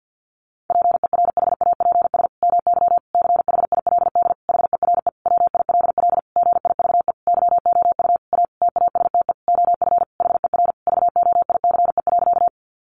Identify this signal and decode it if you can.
Morse code, heard as a keyed tone, about 50 wpm, 700 Hz. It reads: PERHAPS MY CHILD HERE WILL GIVE YOU A TASTE OF HER VOICE.
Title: Sentences from Top 1100 Words 50wpm